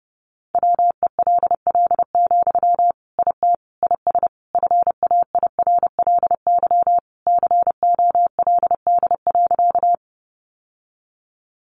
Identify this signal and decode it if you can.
Morse code, heard as a keyed tone, about 28 wpm, 700 Hz. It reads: WELL, IT IS FAIRLY COLD.
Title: Sentences from Top 1100 Words 30wpm